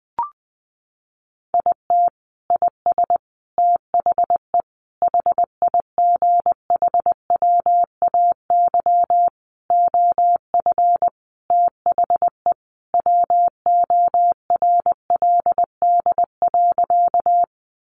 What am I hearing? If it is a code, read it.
Morse code, heard as a keyed tone, about 20 wpm, 700 Hz. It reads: IT IS THE HIGHWAY OF THE WORLD.
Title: Sentences from Top 1400 Words 20wpm